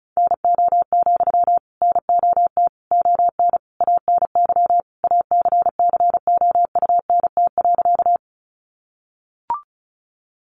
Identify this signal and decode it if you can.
Morse code, heard as a keyed tone, about 34 wpm, 700 Hz. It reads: NO, NOT ON ANY ACCOUNT.
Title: Sentences from Top 900 Words 35wpm